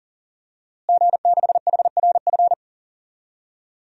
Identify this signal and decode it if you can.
Morse code, heard as a keyed tone, about 40 wpm, 700 Hz. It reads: G6HRF
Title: DXCC-100 Call Signs - 5 Characters - Repeated 2x - 40wpm